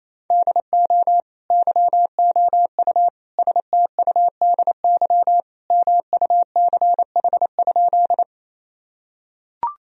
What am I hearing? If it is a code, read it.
Morse code, heard as a keyed tone, about 28 wpm, 700 Hz. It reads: DO YOU STUDY MUCH?
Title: Sentences from Top 1000 Words - Review Part 1 - 28wpm